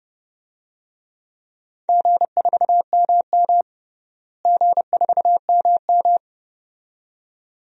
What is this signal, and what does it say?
Morse code, heard as a keyed tone, about 30 wpm, 700 Hz. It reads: G4MM G4MM
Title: DXCC-100 Call Signs - 4 Characters - Repeated 2x - 30wpm